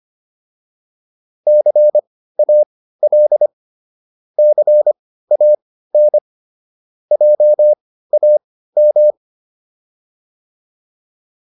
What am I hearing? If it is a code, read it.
Morse code, heard as a keyed tone, about 25 wpm, spaced as for 15 wpm, 600 Hz. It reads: CAL CAN JAM